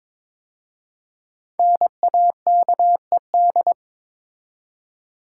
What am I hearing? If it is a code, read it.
Morse code, heard as a keyed tone, about 22 wpm, 700 Hz. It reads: NAKED
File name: 5 Letter Words - First 3 Letters Given - 22wpm